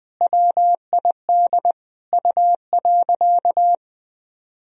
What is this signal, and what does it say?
Morse code, heard as a keyed tone, about 20 wpm, 700 Hz. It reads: WID U.
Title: Sentences from Top 300 Words and Abbreviations 20wpm